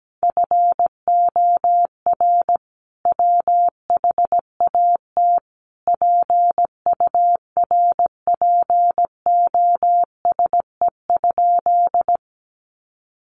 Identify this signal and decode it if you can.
Morse code, heard as a keyed tone, about 17 wpm, 700 Hz. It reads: FOR WHAT PURPOSE?
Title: Sentences from Top 800 Words 17wpm